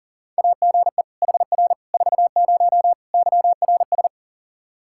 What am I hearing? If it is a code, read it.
Morse code, heard as a keyed tone, about 40 wpm, 700 Hz. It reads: AGE HR 40 YRS
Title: Practice QSO - Exchange 3 - More Info - 40wpm